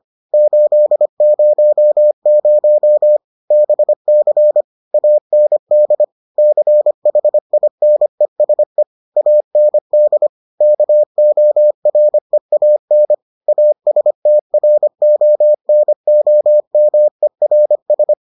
Morse code, heard as a keyed tone, about 25 wpm, 600 Hz. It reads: 800 BC AND CHINESE AND KOREAN ASTRONOMERS